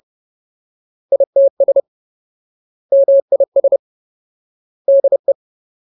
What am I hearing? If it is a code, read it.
Morse code, heard as a keyed tone, about 30 wpm, 550 Hz. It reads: ITS MIS DE
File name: Ultimate Word List / 30WPM Ultimate